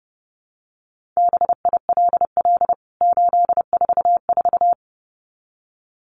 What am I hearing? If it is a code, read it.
Morse code, heard as a keyed tone, about 30 wpm, 700 Hz. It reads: BILL 844
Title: CWT Exchange 30wpm